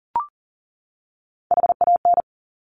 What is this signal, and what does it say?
Morse code, heard as a keyed tone, about 38 wpm, 700 Hz. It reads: HAN
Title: Top Three Letter Combinations in Words 40wpm